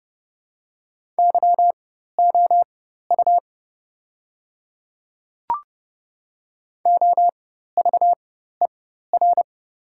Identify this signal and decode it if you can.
Morse code, heard as a keyed tone, about 30 wpm, spaced as for 14 wpm, 690 Hz. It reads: YOU OVER
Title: 4x Character Spacing - Top 100 Words - 30wpm